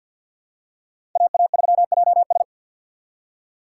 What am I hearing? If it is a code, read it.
Morse code, heard as a keyed tone, about 47 wpm, 700 Hz. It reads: AA3JS
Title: USA Call Signs - 6x Character Spacing - 50wpm